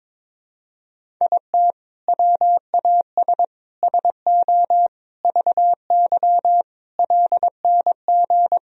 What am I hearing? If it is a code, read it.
Morse code, heard as a keyed tone, about 21 wpm, 700 Hz. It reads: IT WAS SO VY LNG
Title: Sentences from Top 500 Words and Abbreviations 22wpm